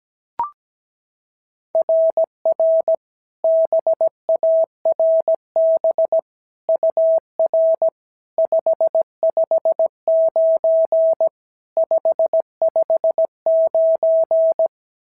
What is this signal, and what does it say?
Morse code, heard as a keyed tone, about 16 wpm, 650 Hz. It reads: RR BARB UR 559 559
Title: SOTA QSO Practice 17wpm